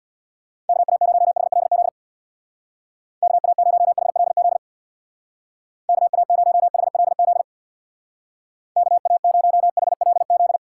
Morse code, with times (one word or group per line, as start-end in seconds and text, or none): 0.7-1.9 KA0HLZ
3.2-4.6 KA0HLZ
5.9-7.4 KA0HLZ
8.8-10.6 KA0HLZ